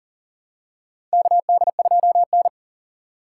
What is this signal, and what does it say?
Morse code, heard as a keyed tone, about 38 wpm, 700 Hz. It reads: KD2N
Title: USA Call Signs - 4 Characters - 3x Character Spacing - 40wpm